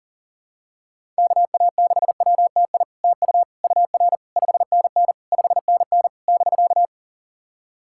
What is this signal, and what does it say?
Morse code, heard as a keyed tone, about 38 wpm, 690 Hz. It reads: KA6WTI TU UR 5NN 5NN <BK>
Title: SOTA QSO Practice 40wpm